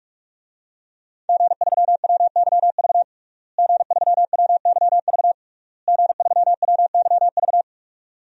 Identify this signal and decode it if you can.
Morse code, heard as a keyed tone, about 46 wpm, 700 Hz. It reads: G3WYV G3WYV G3WYV
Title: DXCC-100 Call Signs - 5 Characters - Repeated 3x - 45wpm